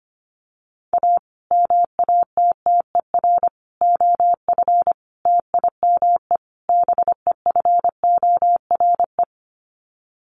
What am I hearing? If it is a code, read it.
Morse code, heard as a keyed tone, about 25 wpm, 700 Hz. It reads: A MATTER OF TIME BEFORE
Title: Sets of 5 Words Encore 25wpm